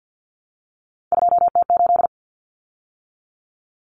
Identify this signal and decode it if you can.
Morse code, heard as a keyed tone, about 48 wpm, 700 Hz. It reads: 2T8
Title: Common Cut Numbers - 3-Digit 50wpm